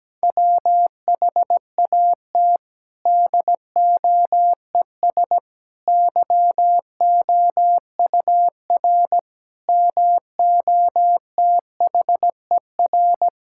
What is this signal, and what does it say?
Morse code, heard as a keyed tone, about 17 wpm, 700 Hz. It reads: WHAT DOES YOUR MOTHER
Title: Sentences from Top 1000 Words - Review Part 5 - 17wpm